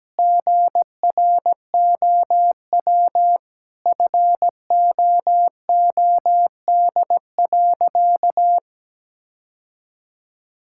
Morse code, heard as a keyed tone, about 17 wpm, 700 Hz. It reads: GROW FOOD.
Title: Sentences from Top 1000 Words - Review Part 6 - 17wpm